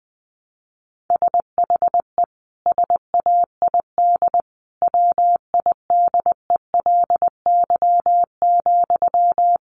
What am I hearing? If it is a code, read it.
Morse code, heard as a keyed tone, about 20 wpm, 700 Hz. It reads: SHE SAID WIDELY,
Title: Sentences from Top 1200 Words 20wpm